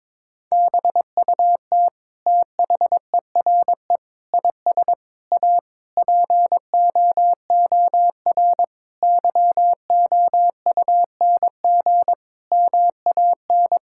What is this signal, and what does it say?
Morse code, heard as a keyed tone, about 22 wpm, 700 Hz. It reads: BUT THERE IS A POOR YOUNG MAN
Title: Sentences from Top 1000 Words - Review Part 6 - 22wpm